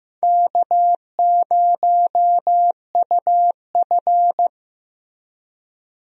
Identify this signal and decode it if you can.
Morse code, heard as a keyed tone, about 15 wpm, 700 Hz. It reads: K0UF
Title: USA Call Signs - 4 Characters - Repeated 3x - 15wpm